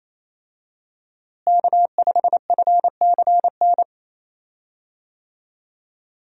Morse code, heard as a keyed tone, about 28 wpm, 700 Hz. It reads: K5FCN
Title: USA Call Signs - 5 Characters - Repeated 3x - 28wpm